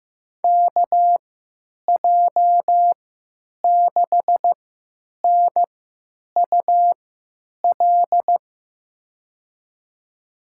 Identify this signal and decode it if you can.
Morse code, heard as a keyed tone, about 15 wpm, spaced as for 9 wpm, 700 Hz. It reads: KJ6NUL
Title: USA Call Signs - 6 Characters - 3x Character Spacing - 15wpm